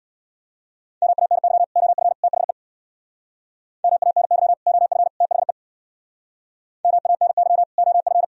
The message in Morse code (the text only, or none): MANY OF THE MANY OF THE MANY OF